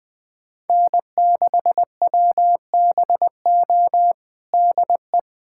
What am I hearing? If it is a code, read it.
Morse code, heard as a keyed tone, about 20 wpm, 700 Hz. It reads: N6WBO DE
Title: USA Callsign DE Callsign - 5 Characters - 20wpm